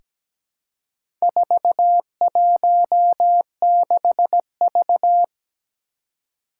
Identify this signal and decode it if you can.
Morse code, heard as a keyed tone, about 17 wpm, 700 Hz. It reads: 416V
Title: Four Challenging Letter-Number Combinations 17wpm